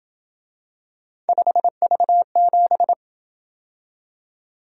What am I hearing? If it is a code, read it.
Morse code, heard as a keyed tone, about 27 wpm, 700 Hz. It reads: 5V7